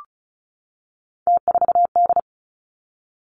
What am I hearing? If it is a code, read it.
Morse code, heard as a keyed tone, about 36 wpm, 700 Hz. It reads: T4D
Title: Three Challenging Letter-Number Combinations 35wpm